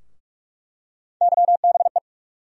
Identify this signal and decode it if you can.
Morse code, heard as a keyed tone, about 45 wpm, 700 Hz. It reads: YBE
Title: USA Call Sign Postfixes - 3 Letters 45wpm